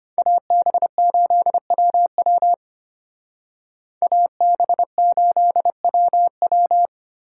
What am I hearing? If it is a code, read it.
Morse code, heard as a keyed tone, about 31 wpm, 700 Hz. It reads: AB8WW AB8WW
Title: Speed Racing - USA 2x2 Call Signs 25wpm